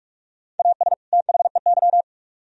Morse code, heard as a keyed tone, about 55 wpm, 700 Hz. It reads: AS THEY
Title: Speed-Racing - Sets of 2 Words 45wpm